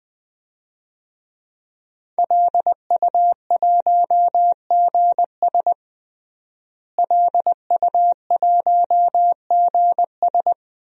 Morse code, heard as a keyed tone, about 20 wpm, 700 Hz. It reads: LU1GS LU1GS
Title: DXCC-100 Call Signs - 5 Characters - Repeated 3x - 20wpm